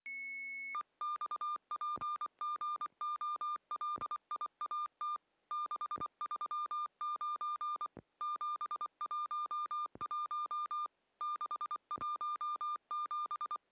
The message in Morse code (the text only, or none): EXPGOLIAT 639 711 617